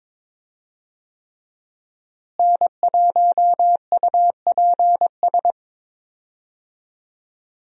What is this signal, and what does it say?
Morse code, heard as a keyed tone, about 22 wpm, 700 Hz. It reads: N1UPS
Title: USA Call Signs - 5 Characters - Repeated 3x - 22wpm